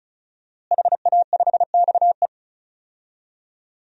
Morse code, heard as a keyed tone, about 36 wpm, 700 Hz. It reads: HA5XE
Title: DXCC-100 Call Signs - 5 Characters - Repeated 2x - 35wpm